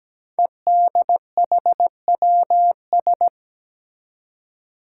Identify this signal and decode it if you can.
Morse code, heard as a keyed tone, about 17 wpm, 700 Hz. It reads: EDHWS
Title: Five Challenging Letter-Number Combinations 17wpm